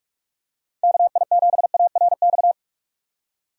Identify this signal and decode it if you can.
Morse code, heard as a keyed tone, about 41 wpm, 700 Hz. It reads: KI7ARX
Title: USA Call Signs - 6 Characters - 45wpm